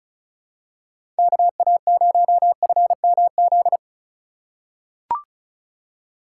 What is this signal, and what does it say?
Morse code, heard as a keyed tone, about 36 wpm, 700 Hz. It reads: KA0FMZ
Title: USA Call Signs - 2x Character Spacing - 35wpm